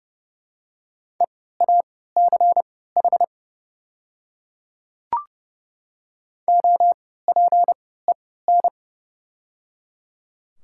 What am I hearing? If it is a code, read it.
Morse code, heard as a keyed tone, about 30 wpm, spaced as for 17 wpm, 700 Hz. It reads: EACH OPEN
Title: 3x Character Spacing - Top 300 Words - 30wpm